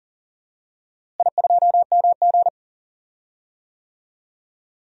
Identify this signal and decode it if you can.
Morse code, heard as a keyed tone, about 40 wpm, 700 Hz. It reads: I2MG
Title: DXCC-100 Call Signs - 4 Characters - Repeated 2x - 40wpm